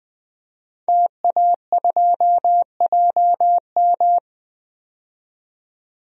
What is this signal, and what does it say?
Morse code, heard as a keyed tone, about 20 wpm, 700 Hz. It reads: TA2JM
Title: DXCC-100 Call Signs - 5 Characters - 20wpm